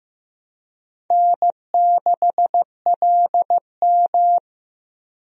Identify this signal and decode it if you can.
Morse code, heard as a keyed tone, about 15 wpm, 700 Hz. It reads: N6LM
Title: USA Call Signs - 4 Characters - 15wpm